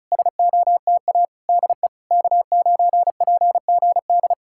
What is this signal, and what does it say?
Morse code, heard as a keyed tone, about 40 wpm, 700 Hz. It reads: SOTA DE K9PGD